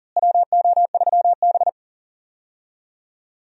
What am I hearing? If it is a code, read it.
Morse code, heard as a keyed tone, about 38 wpm, 700 Hz. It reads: WO3B
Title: USA Call Signs - 4 Characters - Repeated 3x - 40wpm